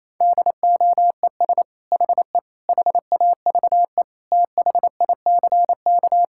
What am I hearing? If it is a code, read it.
Morse code, heard as a keyed tone, about 28 wpm, 700 Hz. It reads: DOES HE HAVE THICK
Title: Sentences from Top 1600 Words 28wpm